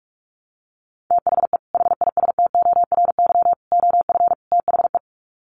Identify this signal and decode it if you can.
Morse code, heard as a keyed tone, about 44 wpm, 700 Hz. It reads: THE HISTORY OF THE
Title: Sets of 4 Words 45wpm